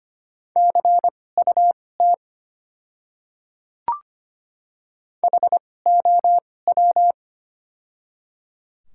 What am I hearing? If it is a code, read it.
Morse code, heard as a keyed tone, about 25 wpm, spaced as for 18 wpm, 700 Hz. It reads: CUT HOW
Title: 2x Character Spacing - Top 300 Words - 25wpm